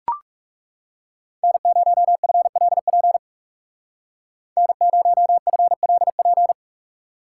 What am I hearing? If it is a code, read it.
Morse code, heard as a keyed tone, about 39 wpm, 700 Hz. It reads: N0FLP N0FLP